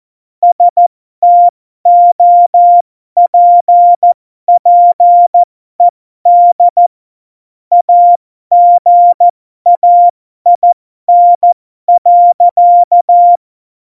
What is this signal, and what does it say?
Morse code, heard as a keyed tone, about 14 wpm, spaced as for 12 wpm, 700 Hz. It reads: STOPPED AGAIN.